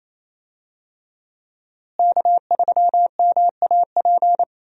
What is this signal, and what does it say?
Morse code, heard as a keyed tone, about 28 wpm, 700 Hz. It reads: K3MAP
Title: USA Call Signs - 5 Characters - Repeated 2x - 28wpm